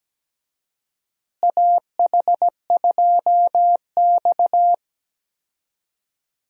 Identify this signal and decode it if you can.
Morse code, heard as a keyed tone, about 17 wpm, 700 Hz. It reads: AH2X
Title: USA Call Signs - 4 Characters - Repeated 2x - 17wpm